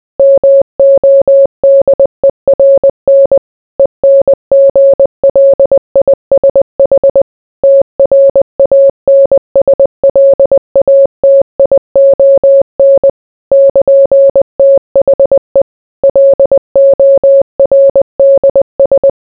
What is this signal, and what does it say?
Morse code, heard as a keyed tone, about 20 wpm, 550 Hz. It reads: MODERN ENGLISH TRANSLATION <KN>THE LORDS